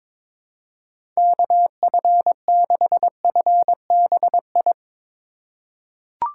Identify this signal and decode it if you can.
Morse code, heard as a keyed tone, about 22 wpm, 700 Hz. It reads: KF6FBI
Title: USA Call Signs - 6 Characters - 2x Character Spacing - 22wpm